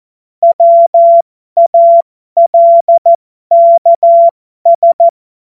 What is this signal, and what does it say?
Morse code, heard as a keyed tone, about 14 wpm, spaced as for 12 wpm, 680 Hz. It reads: WALKS
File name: Morsekurs / Witze